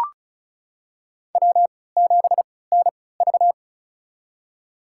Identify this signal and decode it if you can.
Morse code, heard as a keyed tone, about 34 wpm, spaced as for 20 wpm, 700 Hz. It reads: W7NV
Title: USA Call Signs - 4 Characters - 3x Character Spacing - 35wpm